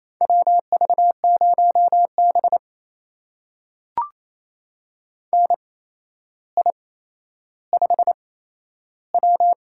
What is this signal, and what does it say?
Morse code, heard as a keyed tone, about 28 wpm, spaced as for 12 wpm, 700 Hz. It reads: WV0B NI5W
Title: USA Call Signs - 8x Character Spacing - 28wpm